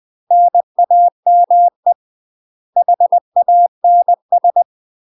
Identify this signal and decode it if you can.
Morse code, heard as a keyed tone, about 20 wpm, 700 Hz. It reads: NAME HANS
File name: UK MASTER QSOs TRAINING / 20WPM / UK MASTER QSOs 0039